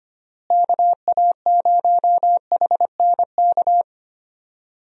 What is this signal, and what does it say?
Morse code, heard as a keyed tone, about 25 wpm, 700 Hz. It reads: KA0HNK
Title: USA Call Signs - 6 Characters - Repeated 3x - 25wpm